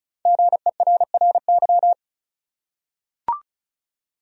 Morse code, heard as a keyed tone, about 36 wpm, 700 Hz. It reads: GERRY
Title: USA Common Names - Max 5 Letters - 35wpm